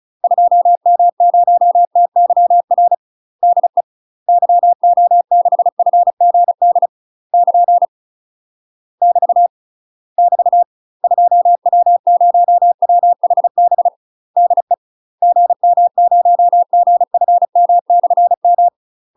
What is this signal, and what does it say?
Morse code, heard as a keyed tone, about 36 wpm, 700 Hz. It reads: 2M0TYR DE YO6FGD <KN> <BT> <BT> 2W0WHB DE GM0GFM/M
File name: UK MASTER QSOs TRAINING / 35WPM / UK MASTER QSOs 0028